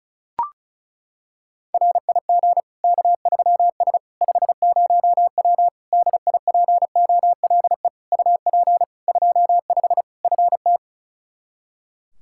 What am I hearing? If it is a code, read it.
Morse code, heard as a keyed tone, about 34 wpm, 700 Hz. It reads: RIG K3S 50W DIPOLE UP 25 FT
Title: Practice QSO - Exchange 2 - Info - 35wpm